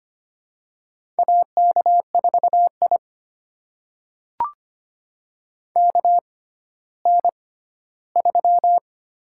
Morse code, heard as a keyed tone, about 25 wpm, spaced as for 14 wpm, 700 Hz. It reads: AK4I KN3
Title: USA Call Signs - 4 Characters - 6x Character Spacing - 25wpm